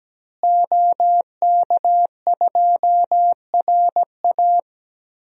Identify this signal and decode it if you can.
Morse code, heard as a keyed tone, about 17 wpm, 700 Hz. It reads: OK2RA
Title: DXCC-100 Call Signs - 5 Characters - Repeated 2x - 17wpm